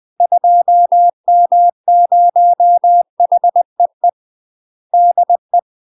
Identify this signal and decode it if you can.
Morse code, heard as a keyed tone, about 20 wpm, 700 Hz. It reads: 2M0HEE DE
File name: UK MASTER QSOs TRAINING / 20WPM / UK MASTER QSOs 0055